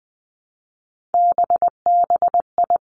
Morse code, heard as a keyed tone, about 20 wpm, 700 Hz. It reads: BBI
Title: Three Challenging Letter-Number Combinations 20wpm